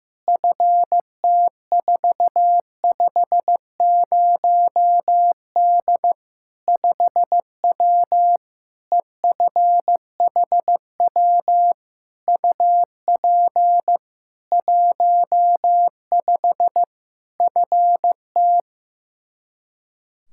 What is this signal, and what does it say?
Morse code, heard as a keyed tone, about 16 wpm, 700 Hz. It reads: FT450D 5W EFHW UP 15 FT
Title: Practice QSO - Exchange 2 - Info - 15wpm